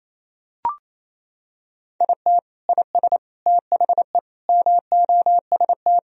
Morse code, heard as a keyed tone, about 28 wpm, 700 Hz. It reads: IT IS THE MOST
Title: Sentences from Top 1000 Words - Review Part 1 - 28wpm